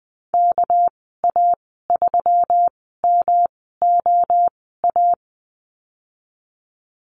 Morse code, heard as a keyed tone, about 20 wpm, spaced as for 15 wpm, 700 Hz. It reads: KA3MOA
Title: 200 Lesson 32g - Letter k - Call Signs 20wpm